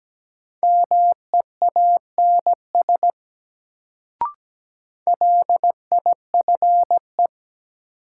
Morse code, heard as a keyed tone, about 17 wpm, 700 Hz. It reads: MEANS LIFE
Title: Sets of 5 Words - New Word Drill - 17wpm